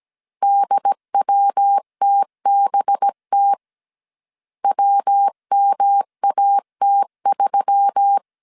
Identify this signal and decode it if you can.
Morse code, heard as a keyed tone, about 17 wpm, 800 Hz. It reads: DWTBT WMAT3
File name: CW / DET / 1 6 1500C S B 16wpm 2619